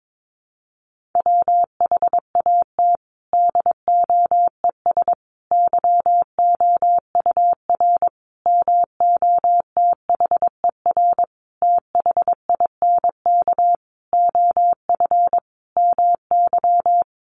Morse code, heard as a keyed tone, about 22 wpm, 700 Hz. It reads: WHAT DOES YOUR MOTHER THINK OF MY
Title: Sentences from Top 1400 Words 22wpm